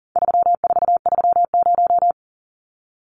Morse code, heard as a keyed tone, about 40 wpm, 700 Hz. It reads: 3430
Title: Numbers - 4-Digit 40wpm